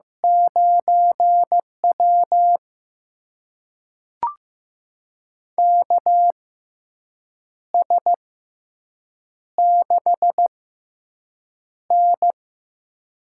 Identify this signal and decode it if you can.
Morse code, heard as a keyed tone, about 15 wpm, spaced as for 6 wpm, 700 Hz. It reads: 9W KS6N
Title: USA Call Signs - 4 Characters - 6x Character Spacing - 15wpm